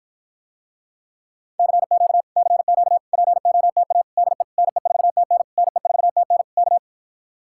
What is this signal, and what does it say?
Morse code, heard as a keyed tone, about 53 wpm, 700 Hz. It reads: CQ CQ POTA DE NE4TN NE4TN K